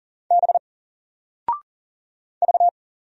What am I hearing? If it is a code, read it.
Morse code, heard as a keyed tone, about 40 wpm, 700 Hz. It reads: B V
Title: Single Letter-Number Alt 40wpm